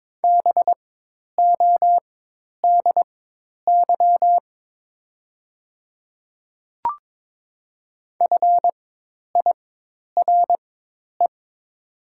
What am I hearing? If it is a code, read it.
Morse code, heard as a keyed tone, about 22 wpm, spaced as for 10 wpm, 700 Hz. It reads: BODY FIRE
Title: 4x Character Spacing - Top 500 Words - 22wpm